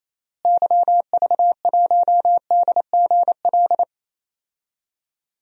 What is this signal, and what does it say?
Morse code, heard as a keyed tone, about 28 wpm, 700 Hz. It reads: YV1DGL
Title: DXCC-100 Call Signs - 6 Characters - Repeated 3x - 28wpm